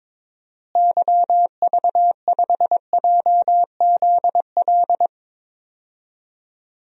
Morse code, heard as a keyed tone, about 22 wpm, 700 Hz. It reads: YV5JZL
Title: DXCC-100 Call Signs - 6 Characters - Repeated 2x - 22wpm